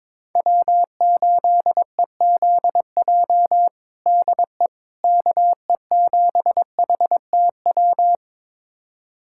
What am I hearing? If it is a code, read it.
Morse code, heard as a keyed tone, about 22 wpm, 700 Hz. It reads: W8EZJ DE KE7HTW
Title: USA Callsign DE Callsign 22wpm